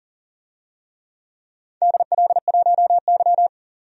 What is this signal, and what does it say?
Morse code, heard as a keyed tone, about 40 wpm, 700 Hz. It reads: DL1Y
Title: DXCC-100 Call Signs - 4 Characters - 40wpm